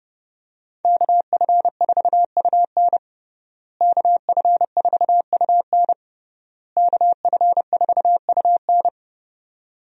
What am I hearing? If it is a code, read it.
Morse code, heard as a keyed tone, about 30 wpm, 700 Hz. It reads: KF4UN KF4UN KF4UN
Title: USA Call Signs - 5 Characters - Repeated 3x - 30wpm